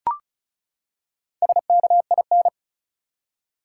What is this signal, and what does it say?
Morse code, heard as a keyed tone, about 34 wpm, 700 Hz. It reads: SKIN
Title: Four Letter Words - 35wpm